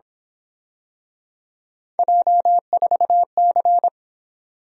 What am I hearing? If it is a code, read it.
Morse code, heard as a keyed tone, about 27 wpm, 700 Hz. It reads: J4C
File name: Speed-Racing - Three Letter-Number Combinations - 22wpm